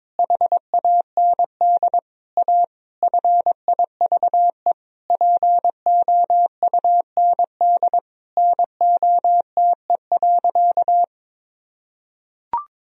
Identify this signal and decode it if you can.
Morse code, heard as a keyed tone, about 21 wpm, 700 Hz. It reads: HAND A FIVE POUND NOTE.
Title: Sentences from Top 1000 Words - Review Part 2 - 22wpm